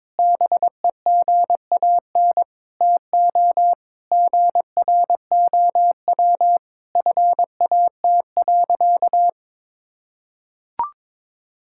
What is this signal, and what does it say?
Morse code, heard as a keyed tone, about 22 wpm, 700 Hz. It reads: BEGAN TO GROW FAT.
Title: Sentences from Top 1000 Words - Review Part 5 - 22wpm